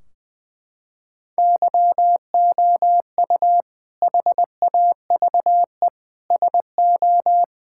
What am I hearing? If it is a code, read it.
Morse code, heard as a keyed tone, about 21 wpm, 700 Hz. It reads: YOU HAVE SO
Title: Sentences from Top 1600 Words 20wpm